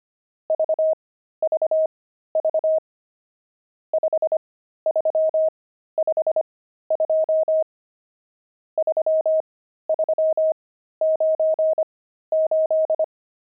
Morse code, heard as a keyed tone, about 25 wpm, spaced as for 13 wpm, 630 Hz. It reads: VVV 5352 3398
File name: pangram-fs-13-cs-25-nums5